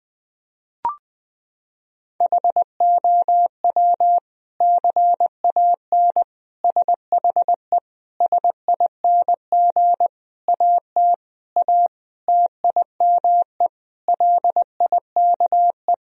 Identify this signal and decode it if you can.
Morse code, heard as a keyed tone, about 20 wpm, 700 Hz. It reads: HOW CAN SHE SING AT A TIME LIKE
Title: Sentences from Top 1000 Words - Review Part 2 - 20wpm